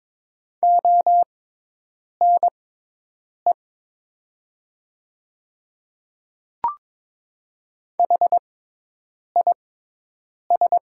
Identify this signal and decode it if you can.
Morse code, heard as a keyed tone, about 22 wpm, spaced as for 8 wpm, 700 Hz. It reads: ONE HIS
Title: 6x Character Spacing - Top 100 Words - 22wpm